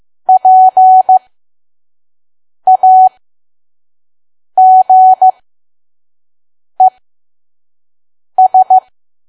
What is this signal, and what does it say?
Morse code, heard as a keyed tone, about 15 wpm, spaced as for 5 wpm, 750 Hz. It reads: PAGES
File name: Archive / 5 WPM files / PM